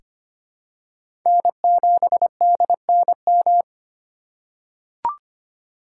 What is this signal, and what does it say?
Morse code, heard as a keyed tone, about 25 wpm, 700 Hz. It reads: N7DNM
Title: USA Call Signs - 5 Characters - 2x Character Spacing - 25wpm